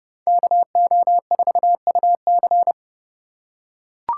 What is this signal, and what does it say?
Morse code, heard as a keyed tone, about 29 wpm, 700 Hz. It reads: KO4UC